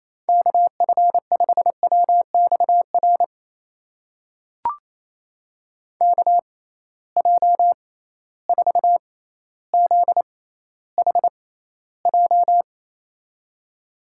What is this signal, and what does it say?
Morse code, heard as a keyed tone, about 28 wpm, spaced as for 14 wpm, 700 Hz. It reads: KF5WXR KJ4ZHJ
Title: USA Call Signs - 6 Characters - 6x Character Spacing - 28wpm